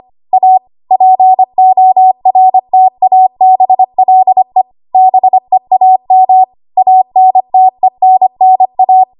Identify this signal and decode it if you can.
Morse code, heard as a keyed tone, about 25 wpm, 750 Hz. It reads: A PORTABLE BEAM ANTENNA
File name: Archive / 25 WPM / 25WPM